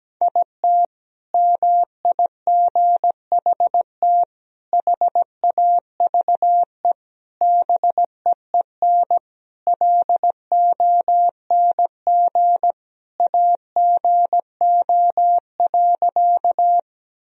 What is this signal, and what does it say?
Morse code, heard as a keyed tone, about 17 wpm, 700 Hz. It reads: IT MIGHT HAVE BEEN LONG AGO.
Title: Sentences from Top 1000 Words - Review Part 3 - 17wpm